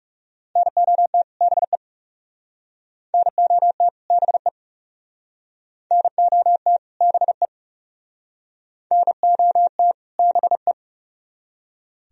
Morse code, heard as a keyed tone, about 39 wpm, 700 Hz. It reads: NOT BE NOT BE NOT BE NOT BE